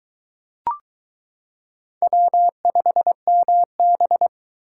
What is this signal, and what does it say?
Morse code, heard as a keyed tone, about 21 wpm, 700 Hz. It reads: W5MB